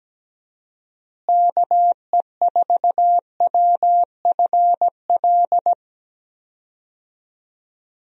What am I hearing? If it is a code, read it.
Morse code, heard as a keyed tone, about 17 wpm, 700 Hz. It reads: KE4WFL
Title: USA Call Signs - 6 Characters - Repeated 3x - 17wpm